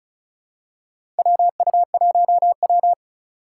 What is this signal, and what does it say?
Morse code, heard as a keyed tone, about 36 wpm, 700 Hz. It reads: WU1W
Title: USA Call Signs - 4 Characters - 4x Character Spacing - 35wpm